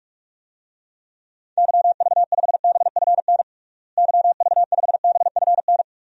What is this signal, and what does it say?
Morse code, heard as a keyed tone, about 43 wpm, 700 Hz. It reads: YV5BFN YV5BFN
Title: DXCC-100 Call Signs - 6 Characters - Repeated 2x - 45wpm